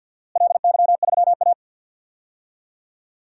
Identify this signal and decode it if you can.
Morse code, heard as a keyed tone, about 50 wpm, 700 Hz. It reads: LY3A
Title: DXCC-100 Call Signs - 4 Characters - Repeated 2x - 50wpm